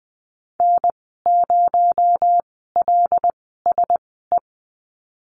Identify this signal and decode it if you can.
Morse code, heard as a keyed tone, about 20 wpm, spaced as for 15 wpm, 700 Hz. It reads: N0LSE
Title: 231 Lesson 37g - Letter x - Call Signs 20wpm